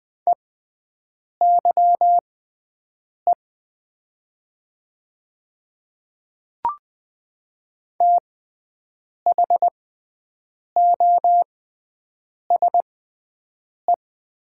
Morse code, heard as a keyed tone, about 20 wpm, spaced as for 7 wpm, 690 Hz. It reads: EYE THOSE
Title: 6x Character Spacing - Top 300 Words - 20wpm